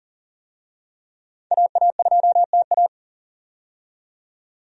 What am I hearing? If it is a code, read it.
Morse code, heard as a keyed tone, about 40 wpm, 700 Hz. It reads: AA2TA
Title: USA Call Signs - 5 Characters - 40wpm